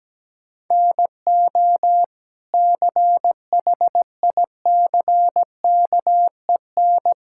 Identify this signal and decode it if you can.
Morse code, heard as a keyed tone, about 17 wpm, 700 Hz. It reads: NO CHICKEN
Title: Sentences from Top 1000 Words - Review Part 5 - 17wpm